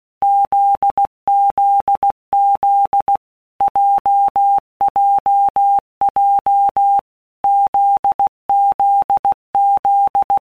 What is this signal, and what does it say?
Morse code, heard as a keyed tone, about 16 wpm, 800 Hz. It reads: ZZZ JJJ ZZZ
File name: Basic Letter Groups By Blocks / Groups@15 / JZ17@15